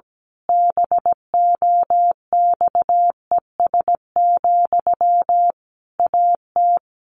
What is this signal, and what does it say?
Morse code, heard as a keyed tone, about 17 wpm, 700 Hz. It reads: BOXES, AT
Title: A Sound Of Thunder 17wpm